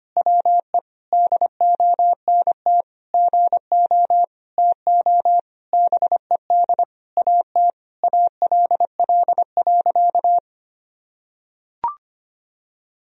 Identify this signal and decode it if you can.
Morse code, heard as a keyed tone, about 25 wpm, 700 Hz. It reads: WE DONT GO TO BED AT ALL.
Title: Sentences from Top 1000 Words - Review Part 3 - 25wpm